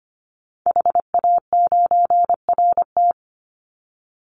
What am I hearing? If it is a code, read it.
Morse code, heard as a keyed tone, about 25 wpm, 700 Hz. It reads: HA9RT
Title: Call Signs - 25wpm